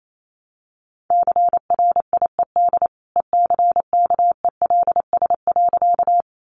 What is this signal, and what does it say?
Morse code, heard as a keyed tone, about 28 wpm, 700 Hz. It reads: CRIED ECKELS.
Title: A Sound Of Thunder 28wpm